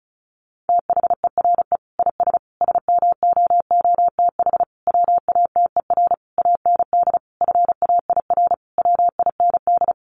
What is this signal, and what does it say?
Morse code, heard as a keyed tone, about 34 wpm, 700 Hz. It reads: THERE IS SMOOTH WATER AND FAIR WIND
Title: Sentences from Top 1400 Words 35wpm